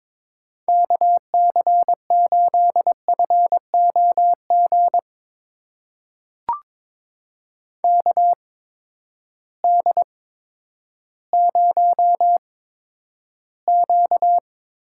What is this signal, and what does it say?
Morse code, heard as a keyed tone, about 22 wpm, spaced as for 11 wpm, 700 Hz. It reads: KC8FOG KD0Q
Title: USA Call Signs - 6 Characters - 8x Character Spacing - 22wpm